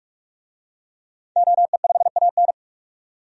Morse code, heard as a keyed tone, about 41 wpm, 700 Hz. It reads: OE5AN
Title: DXCC-100 Call Signs - 5 Characters - 45wpm